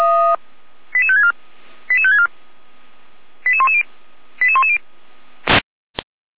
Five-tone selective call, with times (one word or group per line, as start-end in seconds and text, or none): 0.9-1.3 88564
1.9-2.3 88564
3.5-3.8 88110
4.4-4.8 88110